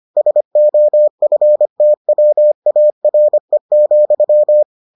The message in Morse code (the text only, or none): SOFTWARE,